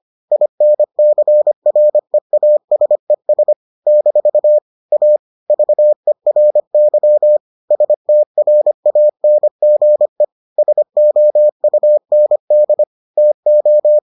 Morse code, heard as a keyed tone, about 25 wpm, 600 Hz. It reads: INCREASES - A VERY STRANGE SOUND TO